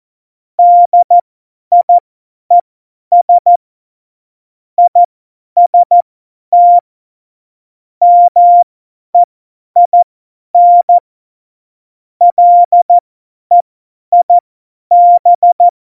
Morse code, heard as a keyed tone, about 14 wpm, spaced as for 10 wpm, 700 Hz. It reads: DIES IST MEIN LEIB
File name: Morsekurs / Witze / Witz 18. Februar 2020, 50 F70